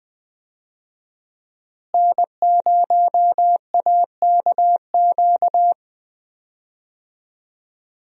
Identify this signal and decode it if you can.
Morse code, heard as a keyed tone, about 20 wpm, 700 Hz. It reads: N0AKQ
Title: USA Call Signs - 5 Characters - Repeated 3x - 20wpm